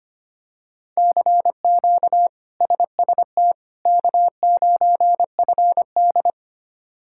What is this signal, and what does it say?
Morse code, heard as a keyed tone, about 25 wpm, 700 Hz. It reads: CQ SST K9FD
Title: K1USN SST QSO Practice - 25wpm